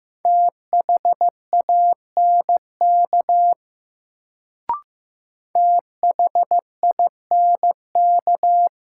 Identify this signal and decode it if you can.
Morse code, heard as a keyed tone, about 15 wpm, 700 Hz. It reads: THANK THINK
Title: Sets of 2 Words - New Word Drill - 15wpm